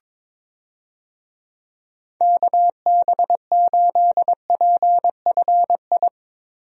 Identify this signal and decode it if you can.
Morse code, heard as a keyed tone, about 22 wpm, 700 Hz. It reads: KB8PFI
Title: USA Call Signs - 6 Characters - Repeated 2x - 22wpm